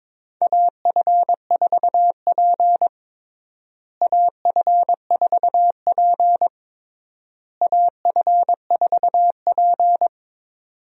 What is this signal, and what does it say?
Morse code, heard as a keyed tone, about 22 wpm, 700 Hz. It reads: AF4P AF4P AF4P
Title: USA Call Signs - 4 Characters - Repeated 3x - 22wpm